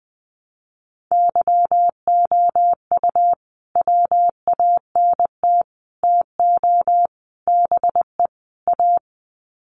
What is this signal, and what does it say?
Morse code, heard as a keyed tone, about 20 wpm, 700 Hz. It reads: YOU WANT TO BE A
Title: Sets of 5 Words Encore2 20wpm